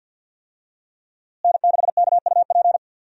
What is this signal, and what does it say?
Morse code, heard as a keyed tone, about 47 wpm, 700 Hz. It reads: N6KUP
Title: USA Call Signs - 5 Characters - Repeated 3x - 50wpm